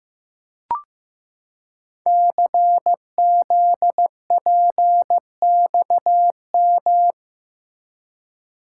Morse code, heard as a keyed tone, about 15 wpm, 700 Hz. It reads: CZPXM
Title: Code Groups - 5 Letters - 15wpm